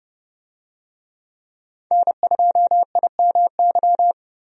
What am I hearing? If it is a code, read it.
Morse code, heard as a keyed tone, about 29 wpm, 700 Hz. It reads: N2IMY
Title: USA Call Signs - 5 Characters - Repeated 2x - 30wpm